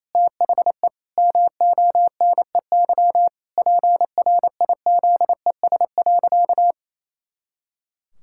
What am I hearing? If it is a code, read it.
Morse code, heard as a keyed tone, about 28 wpm, 700 Hz. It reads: THE MONEY PRIZES.
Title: Pangram Challenge v2 - 28wpm